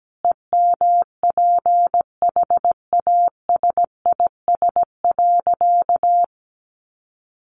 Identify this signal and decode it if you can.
Morse code, heard as a keyed tone, about 17 wpm, 700 Hz. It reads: EMPHASIS.
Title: Sentences from Top 1400 Words 17wpm